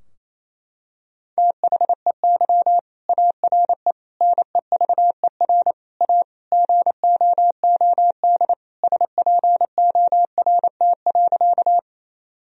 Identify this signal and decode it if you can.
Morse code, heard as a keyed tone, about 28 wpm, 700 Hz. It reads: THEY ARE NEVER A GOOD SPORT.
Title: Sentences from Top 1000 Words - Review Part 4 - 28wpm